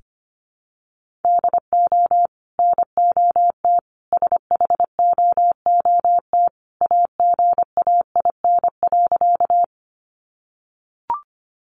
Morse code, heard as a keyed tone, about 25 wpm, 700 Hz. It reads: DO NOT SHOOT AGAIN.
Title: Sentences from Top 1200 Words 25wpm